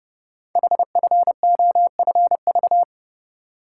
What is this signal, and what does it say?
Morse code, heard as a keyed tone, about 30 wpm, 700 Hz. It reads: HFOFV
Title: Code Groups - 5 Letters - 30wpm